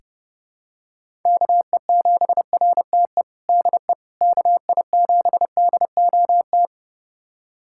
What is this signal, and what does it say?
Morse code, heard as a keyed tone, about 30 wpm, 700 Hz. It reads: KE7RTE DE KI7DOT
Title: USA Callsign DE Callsign - 6 Characters - 30wpm